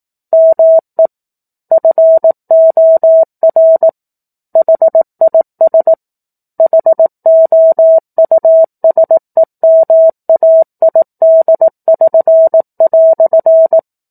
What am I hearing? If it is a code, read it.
Morse code, heard as a keyed tone, about 18 wpm, 650 Hz. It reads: ME FOR HIS HOUSEMAID<SN>"
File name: 04-Chapter-04 18 WPM